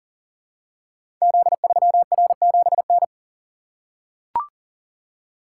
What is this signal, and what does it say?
Morse code, heard as a keyed tone, about 42 wpm, 700 Hz. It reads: Z3R7N
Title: Code Groups - 5 Letter:Numbers - 40wpm